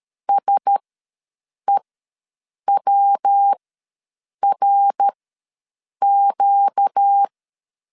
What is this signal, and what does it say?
Morse code, heard as a keyed tone, about 13 wpm, spaced as for 7 wpm, 800 Hz. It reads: SEWRQ
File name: LEZIONI / 8 YJ45 / 8 YJ ws 12wpm 3342